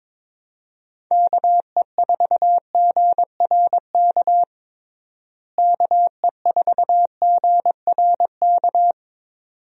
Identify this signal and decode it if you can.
Morse code, heard as a keyed tone, about 22 wpm, 700 Hz. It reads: KE4GRK KE4GRK
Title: USA Call Signs - 6 Characters - Repeated 2x - 22wpm